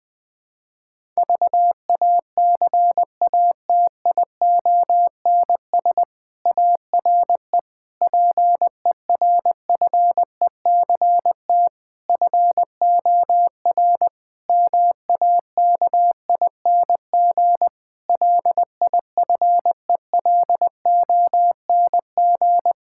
Morse code, heard as a keyed tone, about 21 wpm, 700 Hz. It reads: VACATIONS ARE PERFECT FOR MAKING LIFELONG
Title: Difficult Sentences with Hints - 20wpm